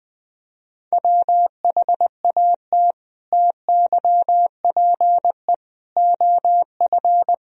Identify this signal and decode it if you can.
Morse code, heard as a keyed tone, about 20 wpm, 700 Hz. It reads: WHAT TYPE OF
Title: Sentences from Top 1000 Words - Review Part 3 - 20wpm